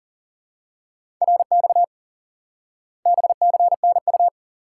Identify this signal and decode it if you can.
Morse code, heard as a keyed tone, about 40 wpm, 700 Hz. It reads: RX BCNU
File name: IQE Course 3 - Lesson 89 - YAGI - Two Elements - 40wpm